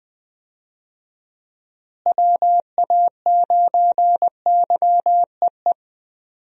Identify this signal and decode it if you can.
Morse code, heard as a keyed tone, about 20 wpm, 700 Hz. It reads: WA9YEE
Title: USA Call Signs - 6 Characters - Repeated 3x - 20wpm